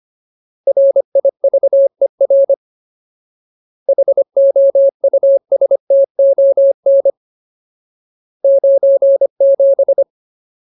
Morse code, heard as a keyed tone, about 25 wpm, 550 Hz. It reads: RIVER HOUSTON 97
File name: Ultimate Word List / 25WPM Ultimate